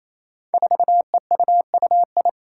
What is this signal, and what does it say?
Morse code, heard as a keyed tone, about 28 wpm, 700 Hz. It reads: 4EUUI
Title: Five Challenging Letter-Number Combinations 28wpm